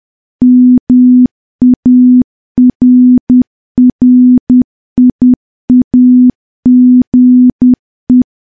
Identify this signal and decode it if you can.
Morse code, heard as a keyed tone, about 10 wpm, 250 Hz. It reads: MARRIAGE
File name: morse / morse-training-10WPM